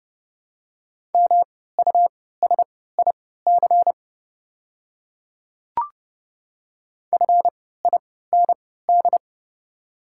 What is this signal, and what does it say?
Morse code, heard as a keyed tone, about 30 wpm, spaced as for 17 wpm, 690 Hz. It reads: MUSIC FIND
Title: 3x Character Spacing - Top 300 Words - 30wpm